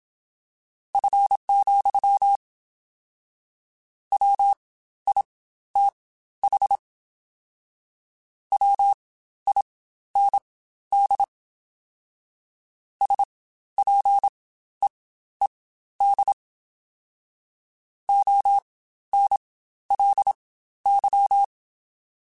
Morse code, heard as a keyed tone, about 27 wpm, spaced as for 13 wpm, 800 Hz. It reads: F, WITH WIND SPEED ONLY
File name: Spaced Characters / Spaced Characters-mp3s / Weather 5 1 sp @25